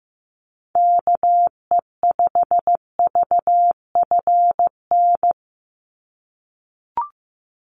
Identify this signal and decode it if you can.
Morse code, heard as a keyed tone, about 15 wpm, 700 Hz. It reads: KE5VFN
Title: USA Call Signs 15wpm